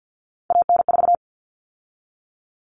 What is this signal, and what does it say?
Morse code, heard as a keyed tone, about 50 wpm, 700 Hz. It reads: AN4
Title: All Cut Numbers - 3-Digit 50wpm